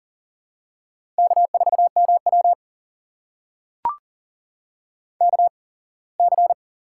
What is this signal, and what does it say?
Morse code, heard as a keyed tone, about 40 wpm, spaced as for 24 wpm, 700 Hz. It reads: K4MW KC
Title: USA Call Signs - 4 Characters - 8x Character Spacing - 40wpm